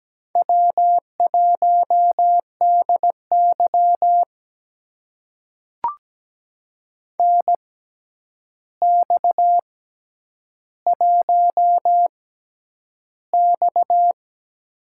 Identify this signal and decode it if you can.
Morse code, heard as a keyed tone, about 17 wpm, spaced as for 9 wpm, 700 Hz. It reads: W1DY NX1X
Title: USA Call Signs - 4 Characters - 6x Character Spacing - 17wpm